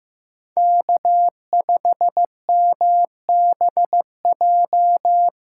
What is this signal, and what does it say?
Morse code, heard as a keyed tone, about 15 wpm, 700 Hz. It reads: K5MBJ
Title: USA Call Signs - 5 Characters - 2x Character Spacing - 15wpm